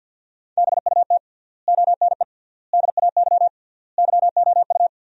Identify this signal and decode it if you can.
Morse code, heard as a keyed tone, about 50 wpm, 700 Hz. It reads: BUT ONE DAY YOU
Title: 3x Word-Spacing - Sentences from Top 400 Words - 50wpm